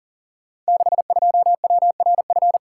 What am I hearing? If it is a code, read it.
Morse code, heard as a keyed tone, about 39 wpm, 690 Hz. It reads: 62WRF
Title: Code Groups - 5 Letter:Numbers - 40wpm